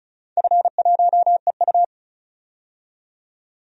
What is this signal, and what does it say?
Morse code, heard as a keyed tone, about 36 wpm, 700 Hz. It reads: F1EU
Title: DXCC-100 Call Signs - 4 Characters - 35wpm